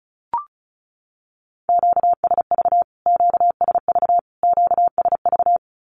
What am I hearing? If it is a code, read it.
Morse code, heard as a keyed tone, about 34 wpm, 700 Hz. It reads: QSV QSV QSV
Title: Q-Codes Repeated 3x 35wpm